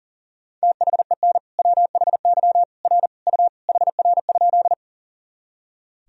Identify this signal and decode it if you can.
Morse code, heard as a keyed tone, about 40 wpm, 700 Hz. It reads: THEN WHY R U HR?
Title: Sentences from Top 200 Words and Abbreviations 40wpm